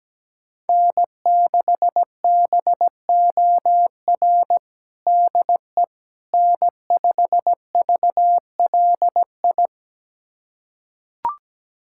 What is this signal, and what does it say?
Morse code, heard as a keyed tone, about 17 wpm, 700 Hz. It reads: N6BOR DE N5VLI
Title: USA Callsign DE Callsign - 5 Characters - 17wpm